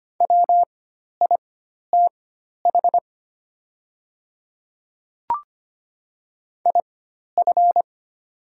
Morse code, heard as a keyed tone, about 25 wpm, spaced as for 12 wpm, 700 Hz. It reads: WITH IF
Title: 4x Character Spacing - Top 300 Words - 25wpm